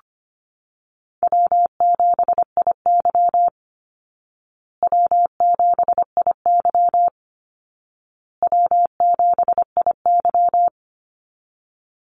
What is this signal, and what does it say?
Morse code, heard as a keyed tone, about 25 wpm, 700 Hz. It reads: W7IY W7IY W7IY
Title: Easy Call Signs Repeated 3x - 25wpm